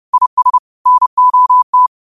Morse code, heard as a keyed tone, about 30 wpm, 1000 Hz. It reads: IS NOT
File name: qotdfiles / 30wpm